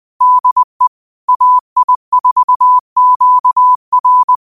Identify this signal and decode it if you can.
Morse code, heard as a keyed tone, about 20 wpm, 1000 Hz. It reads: DE AI4QR
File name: qotdfiles / 20wpm